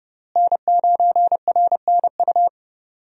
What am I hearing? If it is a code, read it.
Morse code, heard as a keyed tone, about 30 wpm, 700 Hz. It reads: N9RNU
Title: USA Call Signs - 5 Characters - Repeated 2x - 30wpm